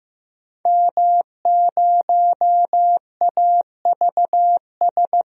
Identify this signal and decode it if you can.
Morse code, heard as a keyed tone, about 15 wpm, 700 Hz. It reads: M0AVS
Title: DXCC-100 Call Signs - 5 Characters - Repeated 3x - 15wpm